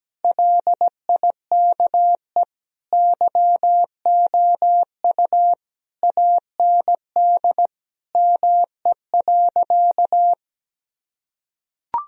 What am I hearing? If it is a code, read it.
Morse code, heard as a keyed tone, about 17 wpm, 700 Hz. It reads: LIKE YOU AND ME.
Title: Sentences from Top 1000 Words - Review Part 1 - 17wpm